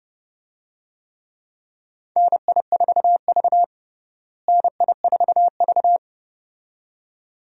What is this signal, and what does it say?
Morse code, heard as a keyed tone, about 31 wpm, 700 Hz. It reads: NI4V NI4V
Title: USA Call Signs - 4 Characters - Repeated 2x - 30wpm